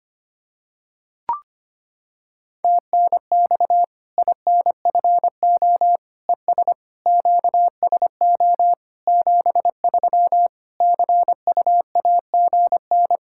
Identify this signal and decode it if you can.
Morse code, heard as a keyed tone, about 25 wpm, 700 Hz. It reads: TNX INFO ES QSO 73 CUAGN
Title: K1USN SST QSO Practice 25wpm